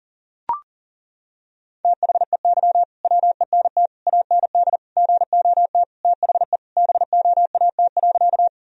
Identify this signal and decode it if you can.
Morse code, heard as a keyed tone, about 37 wpm, 700 Hz. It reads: THEY WENT AND GOT THE BOAT.
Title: Sentences from Top 1000 Words - Review Part 3 - 40wpm